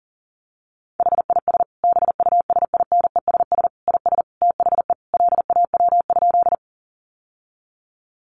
Morse code, heard as a keyed tone, about 40 wpm, 700 Hz. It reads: HIS BUSINESS IS THE LAW?
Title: Sentences from Top 600 Words 40wpm